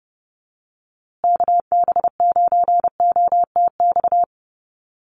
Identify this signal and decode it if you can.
Morse code, heard as a keyed tone, about 30 wpm, 700 Hz. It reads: KB9OTX
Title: USA Call Signs Repeated 2x 30wpm